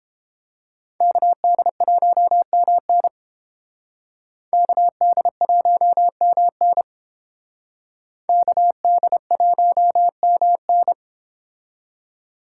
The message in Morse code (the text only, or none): KD1MN KD1MN KD1MN